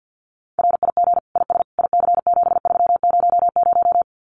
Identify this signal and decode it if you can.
Morse code, heard as a keyed tone, about 50 wpm, 700 Hz. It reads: RIG IS IC7300
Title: Common QSO Phrases Encore 50wpm